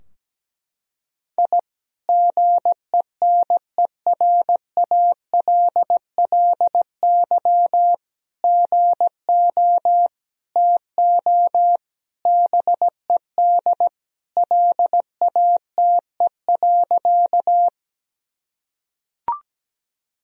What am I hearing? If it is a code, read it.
Morse code, heard as a keyed tone, about 17 wpm, 700 Hz. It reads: I GENERALLY GO TO BED LATE.
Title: Sentences from Top 1000 Words - Review Part 4 - 17wpm